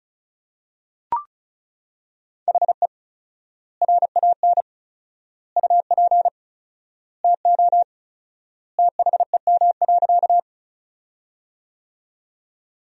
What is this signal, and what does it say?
Morse code, heard as a keyed tone, about 36 wpm, 700 Hz. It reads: HE RAN UP TO THEM.
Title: 4x Word-Spacing - Sentences from Top 500 Words - 35wpm